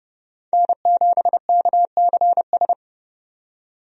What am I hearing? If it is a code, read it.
Morse code, heard as a keyed tone, about 29 wpm, 700 Hz. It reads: N7KCS
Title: USA Call Signs - 5 Characters - 30wpm